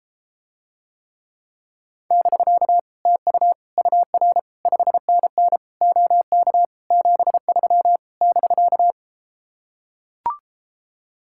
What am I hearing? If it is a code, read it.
Morse code, heard as a keyed tone, about 33 wpm, 700 Hz. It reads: <BK> TU UR 5NN OK 73 <BK>